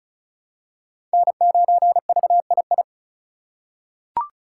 Morse code, heard as a keyed tone, about 34 wpm, 700 Hz. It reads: N9VII
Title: USA Call Signs - 5 Characters - 4x Character Spacing - 35wpm